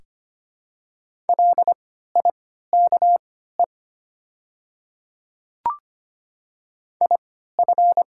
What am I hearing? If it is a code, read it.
Morse code, heard as a keyed tone, about 25 wpm, spaced as for 14 wpm, 700 Hz. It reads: LIKE IF
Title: 3x Character Spacing - Top 100 Words - 25wpm